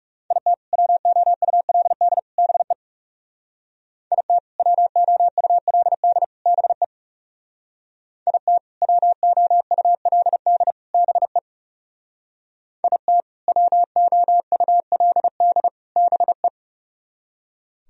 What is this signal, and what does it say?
Morse code, heard as a keyed tone, about 37 wpm, 700 Hz. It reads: IT WOULD BE IT WOULD BE IT WOULD BE IT WOULD BE
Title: Speed-Racing - Sets of 3 Words 30wpm